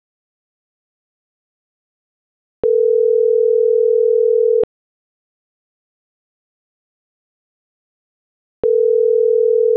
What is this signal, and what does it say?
Ringback tone (North America), 440 + 480 Hz.
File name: CENTRAL GENERAL / out ricardo.diaz